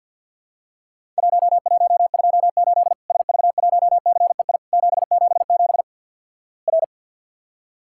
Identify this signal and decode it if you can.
Morse code, heard as a keyed tone, about 50 wpm, 700 Hz. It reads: 1128 SV1CEI 777 R
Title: Worked All Europe DX Contest - QTC Exchange with Repeats - 50wpm